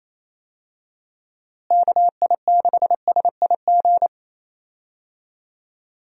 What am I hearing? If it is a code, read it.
Morse code, heard as a keyed tone, about 27 wpm, 700 Hz. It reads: KI6SIG
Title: USA Call Signs - 6 Characters - Repeated 3x - 28wpm